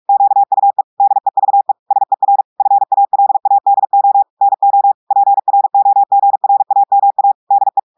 Morse code, heard as a keyed tone, about 46 wpm, 800 Hz. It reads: QUE DEVE SER FALADO NO PROGRAMA DE
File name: audios / portugais / portugais texte en clair 2 vitesse 45 mots minutes